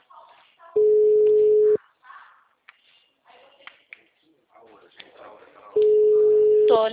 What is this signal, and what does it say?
Ringback tone (Europe, 425 Hz), with speech.